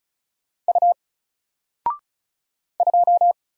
Morse code, heard as a keyed tone, about 36 wpm, 700 Hz. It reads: U 2
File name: Single-Letter-Number-35wpm